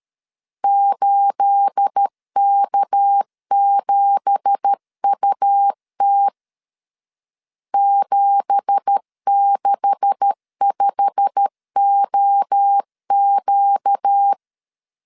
Morse code, heard as a keyed tone, about 13 wpm, 800 Hz. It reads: 8K7UT 765OQ
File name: LEZIONI / 9 2867 12wpm 2322